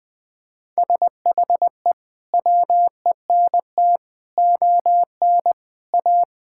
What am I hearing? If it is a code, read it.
Morse code, heard as a keyed tone, about 21 wpm, 700 Hz. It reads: SHE WENT ON A
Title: Sentences with 2-Word Phrases - 20wpm